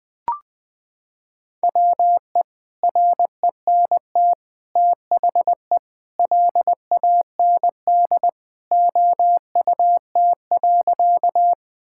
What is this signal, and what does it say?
Morse code, heard as a keyed tone, about 20 wpm, 700 Hz. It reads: WE RENT THE LAND OUT.
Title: Sentences from Top 1000 Words - Review Part 5 - 20wpm